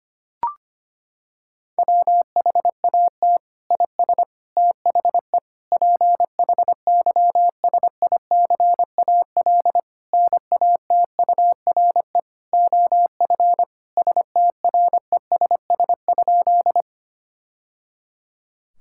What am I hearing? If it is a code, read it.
Morse code, heard as a keyed tone, about 25 wpm, 700 Hz. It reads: WHAT IS THE PHYSICAL NATURE OF STRESS?
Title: Sentences from Top 1000 Words - Review Part 6 - 25wpm